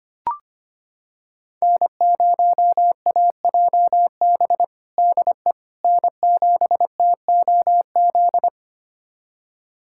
Morse code, heard as a keyed tone, about 25 wpm, 700 Hz. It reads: N0AJB DE N7TOZ
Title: USA Callsign DE Callsign - 5 Characters - 25wpm